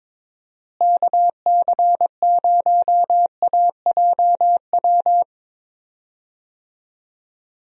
Morse code, heard as a keyed tone, about 23 wpm, 700 Hz. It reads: KC0AJW
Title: USA Call Signs - 6 Characters - Repeated 3x - 22wpm